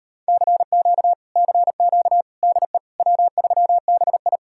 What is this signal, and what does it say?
Morse code, heard as a keyed tone, about 38 wpm, 700 Hz. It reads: CQ CQ DE W3BI